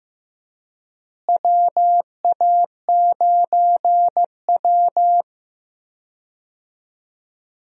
Morse code, heard as a keyed tone, about 15 wpm, 700 Hz. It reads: WA9W
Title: USA Call Signs - 4 Characters - Repeated 3x - 15wpm